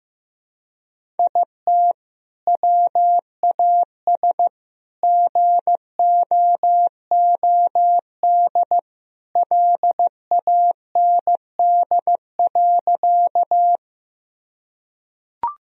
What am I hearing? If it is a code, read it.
Morse code, heard as a keyed tone, about 15 wpm, 700 Hz. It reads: IT WAS GOOD LAND.
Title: Sentences from Top 1000 Words - Review Part 1 - 15wpm